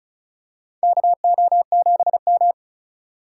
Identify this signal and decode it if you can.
Morse code, heard as a keyed tone, about 36 wpm, 700 Hz. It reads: KO7M
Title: USA Call Signs - 4 Characters - 4x Character Spacing - 35wpm